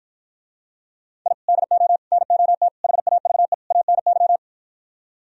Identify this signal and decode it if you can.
Morse code, heard as a keyed tone, about 59 wpm, 700 Hz. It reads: I DO NOT HAVE ANY